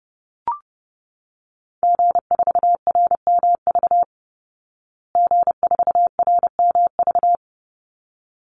Morse code, heard as a keyed tone, about 30 wpm, 700 Hz. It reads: G4RMV G4RMV
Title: Call Signs Repeated 3x - 30wpm